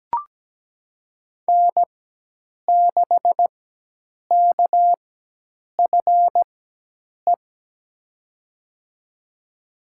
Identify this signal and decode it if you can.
Morse code, heard as a keyed tone, about 17 wpm, spaced as for 8 wpm, 700 Hz. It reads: N6KFE
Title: USA Call Signs - 5 Characters - 4x Character Spacing - 17wpm